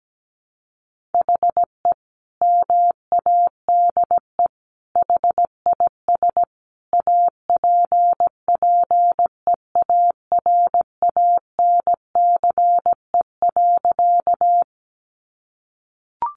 Morse code, heard as a keyed tone, about 17 wpm, 700 Hz. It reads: HE MADE HIS APPEARANCE.
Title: Sentences from Top 1100 Words 17wpm